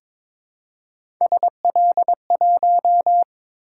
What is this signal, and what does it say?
Morse code, heard as a keyed tone, about 22 wpm, 700 Hz. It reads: SL1
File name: practice / 22wpm / Three Letter-Number Combinations - 22wpm